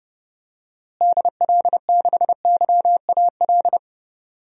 Morse code, heard as a keyed tone, about 30 wpm, 700 Hz. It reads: DL6YAL
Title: DXCC-100 Call Signs - 6 Characters - Repeated 3x - 30wpm